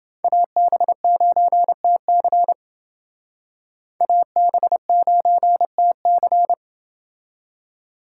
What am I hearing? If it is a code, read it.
Morse code, heard as a keyed tone, about 28 wpm, 700 Hz. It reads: AB9TC AB9TC